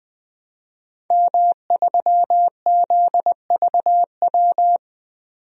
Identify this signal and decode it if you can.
Morse code, heard as a keyed tone, about 20 wpm, 700 Hz. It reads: M3ZVW
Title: DXCC-100 Call Signs - 5 Characters - 20wpm